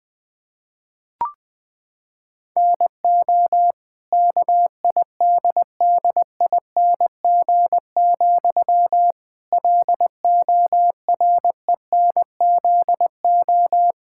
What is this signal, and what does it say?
Morse code, heard as a keyed tone, about 20 wpm, 700 Hz. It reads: NO KIDDING, LORENZO
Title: Pangram Challenge v2 - 20wpm